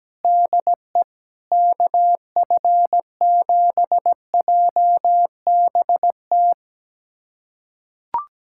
Morse code, heard as a keyed tone, about 17 wpm, 700 Hz. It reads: DE KF7JBT
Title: USA Callsign DE Callsign 17wpm